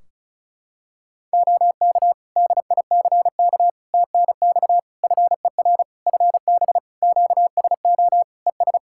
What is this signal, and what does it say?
Morse code, heard as a keyed tone, about 36 wpm, 700 Hz. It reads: OK DICK TNX FER FB QSO ES
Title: Practice QSO - Exchange 4 - Ending - 35wpm